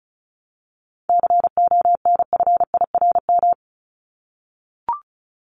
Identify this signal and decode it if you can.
Morse code, heard as a keyed tone, about 34 wpm, 700 Hz. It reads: CONFIRM
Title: Top 1751-2000 Words 35wpm